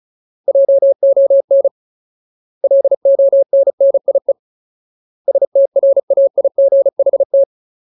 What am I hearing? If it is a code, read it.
Morse code, heard as a keyed tone, about 34 wpm, 550 Hz. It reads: JON LONNIE STRAIGHT
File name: Ultimate Word List / 35WPM Ultimate